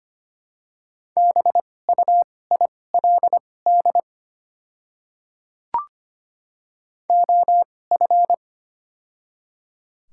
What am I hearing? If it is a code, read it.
Morse code, heard as a keyed tone, about 25 wpm, spaced as for 18 wpm, 700 Hz. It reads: BUILD OF
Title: 2x Character Spacing - Top 300 Words - 25wpm